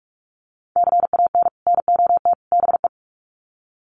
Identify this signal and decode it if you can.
Morse code, heard as a keyed tone, about 44 wpm, 700 Hz. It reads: CAN NOT BE
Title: Sets of 3 Words 45wpm